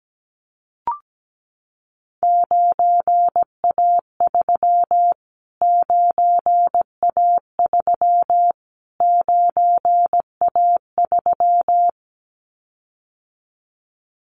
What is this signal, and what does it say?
Morse code, heard as a keyed tone, about 17 wpm, 700 Hz. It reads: 9A3 9A3 9A3
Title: Callsign Prefixes for Top 25 DXCC - Repeated 3x - 17wpm